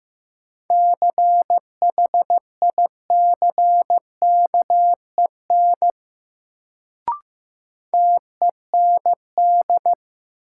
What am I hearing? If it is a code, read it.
Morse code, heard as a keyed tone, about 15 wpm, 700 Hz. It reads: CHICKEN TEND
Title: Sentences from Top 1000 Words - Review Part 5 - New Word Drill - 15wpm